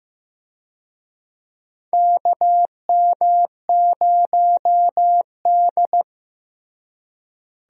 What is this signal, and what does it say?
Morse code, heard as a keyed tone, about 15 wpm, 700 Hz. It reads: KM0D
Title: USA Call Signs - 4 Characters - Repeated 3x - 15wpm